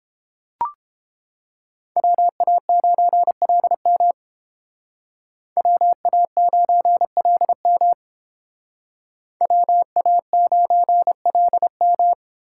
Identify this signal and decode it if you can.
Morse code, heard as a keyed tone, about 30 wpm, 700 Hz. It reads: WA9LM WA9LM WA9LM